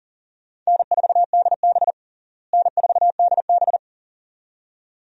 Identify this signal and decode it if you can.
Morse code, heard as a keyed tone, about 40 wpm, 700 Hz. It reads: N4DB N4DB
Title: USA Call Signs - 4 Characters - Repeated 2x - 40wpm